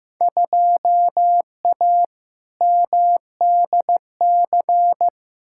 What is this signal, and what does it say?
Morse code, heard as a keyed tone, about 15 wpm, 700 Hz. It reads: 2A MDC
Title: Field Day Exchanges - 15wpm